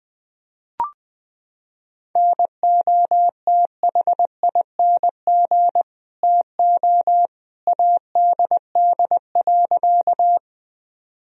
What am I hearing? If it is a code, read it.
Morse code, heard as a keyed tone, about 20 wpm, 700 Hz. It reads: NOTHING TO ADD.
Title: Sentences from Top 1000 Words - Review Part 1 - 20wpm